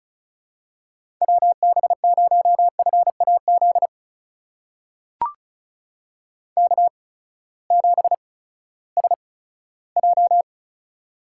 Morse code, heard as a keyed tone, about 36 wpm, spaced as for 18 wpm, 700 Hz. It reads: WB0FAZ K7SJ
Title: USA Call Signs - 8x Character Spacing - 35wpm